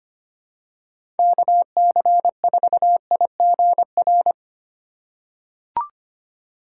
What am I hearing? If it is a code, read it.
Morse code, heard as a keyed tone, about 25 wpm, 700 Hz. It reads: KC4IGR
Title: USA Call Signs - 6 Characters - 2x Character Spacing - 25wpm